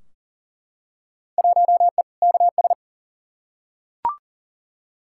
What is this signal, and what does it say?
Morse code, heard as a keyed tone, about 38 wpm, 700 Hz. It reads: 1E KS
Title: Field Day Exchanges - 40wpm